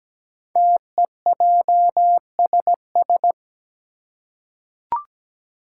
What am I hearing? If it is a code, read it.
Morse code, heard as a keyed tone, about 17 wpm, 700 Hz. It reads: TEJSS
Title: Code Groups - 5 Letters - 17wpm